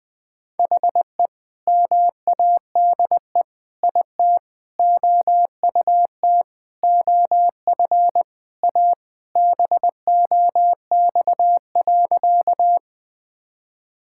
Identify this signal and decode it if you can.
Morse code, heard as a keyed tone, about 21 wpm, 700 Hz. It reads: HE MADE IT OUT OF A BOX.
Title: Sentences with 3-Word Phrases - 20wpm